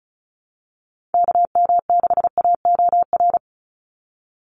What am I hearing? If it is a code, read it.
Morse code, heard as a keyed tone, about 36 wpm, 700 Hz. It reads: KM6AOR
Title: USA Call Signs 35wpm